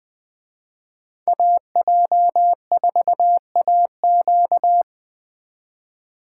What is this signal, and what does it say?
Morse code, heard as a keyed tone, about 20 wpm, 700 Hz. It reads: AJ4AQ